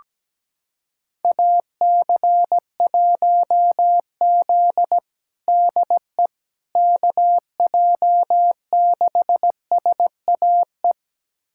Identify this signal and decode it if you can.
Morse code, heard as a keyed tone, about 17 wpm, 700 Hz. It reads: AC1Z DE KJ6SAE
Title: USA Callsign DE Callsign 17wpm